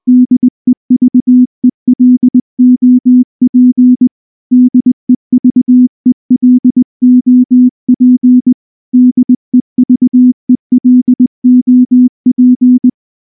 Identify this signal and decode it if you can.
Morse code, heard as a keyed tone, about 19 wpm, 250 Hz. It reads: DEVELOP DEVELOP DEVELOP
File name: morse / morse-training-20WPM-4